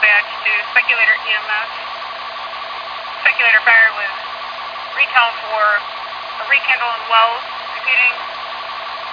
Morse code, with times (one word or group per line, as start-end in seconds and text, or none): none